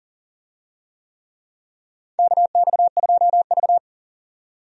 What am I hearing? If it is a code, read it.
Morse code, heard as a keyed tone, about 37 wpm, 700 Hz. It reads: KX2V
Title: USA Call Signs - 4 Characters - 40wpm